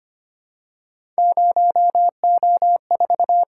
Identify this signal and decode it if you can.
Morse code, heard as a keyed tone, about 25 wpm, 700 Hz. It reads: 0O4
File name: Three Letter-Number Combinations - 25wpm